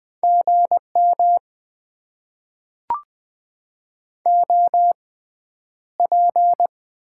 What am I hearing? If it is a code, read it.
Morse code, heard as a keyed tone, about 20 wpm, spaced as for 10 wpm, 700 Hz. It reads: GM OP
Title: Top 90 QSO Elements - 6x Character Spacing - 20wpm